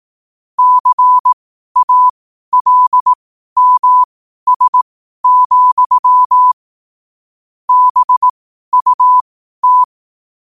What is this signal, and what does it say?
Morse code, heard as a keyed tone, about 18 wpm, spaced as for 12 wpm, 1000 Hz. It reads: CALMS, BUT
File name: qotdfiles / 12wpm